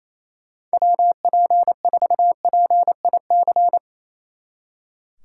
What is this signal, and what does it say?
Morse code, heard as a keyed tone, about 35 wpm, 700 Hz. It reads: WP4PIC
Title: Speed Racing - USA 2x3 Call Signs 28wpm